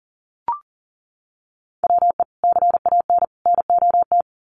Morse code, heard as a keyed tone, about 40 wpm, 700 Hz. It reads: WE CAN NOT
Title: Sets of 3 Words 40wpm